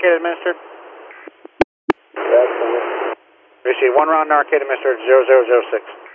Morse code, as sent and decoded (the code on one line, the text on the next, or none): none